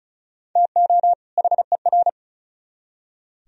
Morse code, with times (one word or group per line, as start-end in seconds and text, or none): 0.6-1.1 TO
1.4-2.1 HER